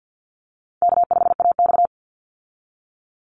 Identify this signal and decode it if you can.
Morse code, heard as a keyed tone, about 50 wpm, 700 Hz. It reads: K5AX
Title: Call Signs - 50wpm